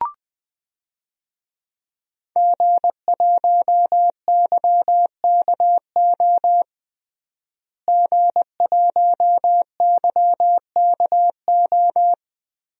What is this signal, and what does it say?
Morse code, heard as a keyed tone, about 20 wpm, 700 Hz. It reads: G1YKO G1YKO
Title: DXCC-100 Call Signs - 5 Characters - Repeated 3x - 20wpm